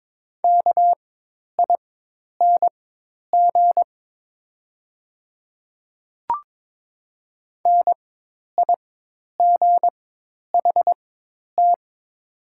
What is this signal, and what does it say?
Morse code, heard as a keyed tone, about 22 wpm, spaced as for 10 wpm, 700 Hz. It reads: KING NIGHT
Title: 4x Character Spacing - Top 500 Words - 22wpm